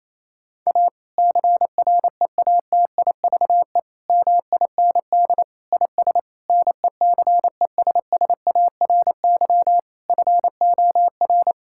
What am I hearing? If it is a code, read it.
Morse code, heard as a keyed tone, about 28 wpm, 700 Hz. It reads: A CREATIVE MIND IS NECESSARY FOR
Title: Sentences from Top 1000 Words - Review Part 5 - 28wpm